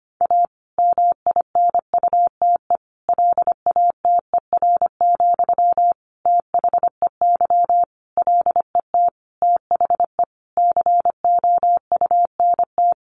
Morse code, heard as a keyed tone, about 25 wpm, 700 Hz. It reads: A MINUTE LATER, THEY LET THE COUNT
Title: Sentences from Top 1500 Words 25wpm